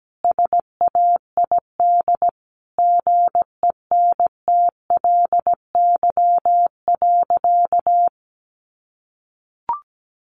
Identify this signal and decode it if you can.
Morse code, heard as a keyed tone, about 17 wpm, 700 Hz. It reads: SAID GENTLY.
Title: Sentences from Top 1500 Words 17wpm